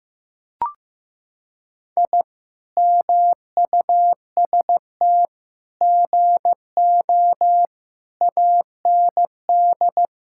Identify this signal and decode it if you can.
Morse code, heard as a keyed tone, about 15 wpm, 700 Hz. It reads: I MUST GO AND
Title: Sentences from Top 1000 Words - Review Part 5 - 15wpm